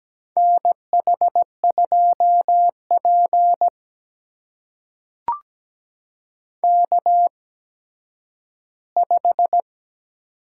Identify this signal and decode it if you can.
Morse code, heard as a keyed tone, about 17 wpm, spaced as for 10 wpm, 700 Hz. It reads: NH2P K5
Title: USA Call Signs - 8x Character Spacing - 17wpm